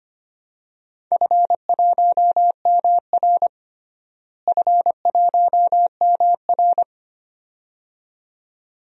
Morse code, heard as a keyed tone, about 25 wpm, 700 Hz. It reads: F1MR F1MR
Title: DXCC-100 Call Signs - 4 Characters - Repeated 2x - 25wpm